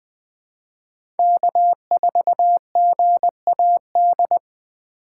Morse code, heard as a keyed tone, about 20 wpm, 700 Hz. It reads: K4GAD
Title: USA Call Signs - 5 Characters - 3x Character Spacing - 20wpm